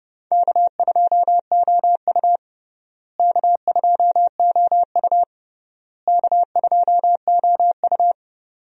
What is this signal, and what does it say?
Morse code, heard as a keyed tone, about 30 wpm, 700 Hz. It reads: K2OU K2OU K2OU
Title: USA Call Signs - 4 Characters - Repeated 3x - 30wpm